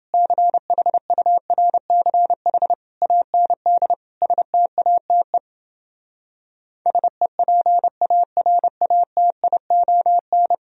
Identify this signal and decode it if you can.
Morse code, heard as a keyed tone, about 28 wpm, 700 Hz. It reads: CHURCH AND STATE SEPARATION